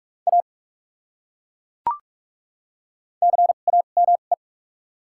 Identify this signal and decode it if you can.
Morse code, heard as a keyed tone, about 42 wpm, spaced as for 31 wpm, 700 Hz. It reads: A CAME
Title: 2x Character Spacing - Top 200 Words - 45wpm